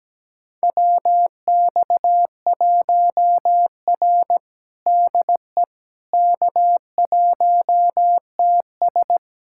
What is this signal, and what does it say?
Morse code, heard as a keyed tone, about 17 wpm, 700 Hz. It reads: WX1R DE K1TS
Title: USA Callsign DE Callsign - 4 Characters - 17wpm